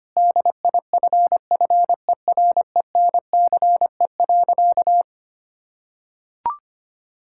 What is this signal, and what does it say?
Morse code, heard as a keyed tone, about 25 wpm, 700 Hz. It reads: DIFFERENCE.
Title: Sentences from Top 1000 Words - Review Part 5 - 25wpm